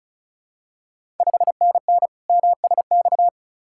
Morse code, heard as a keyed tone, about 36 wpm, 690 Hz. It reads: 5NN MSX
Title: Ontario QSO Party Exchange 35wpm